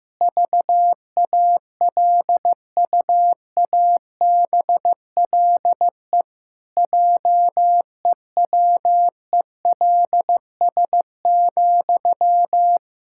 Morse code, heard as a keyed tone, about 15 wpm, 700 Hz. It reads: VALUABLE JEWELS,
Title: Pangram Challenge v2 - 15wpm